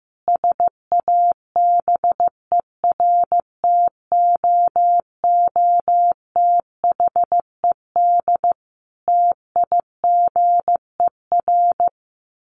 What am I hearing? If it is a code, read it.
Morse code, heard as a keyed tone, about 15 wpm, 700 Hz. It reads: SABERTOOTHED TIGER
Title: A Sound Of Thunder 15wpm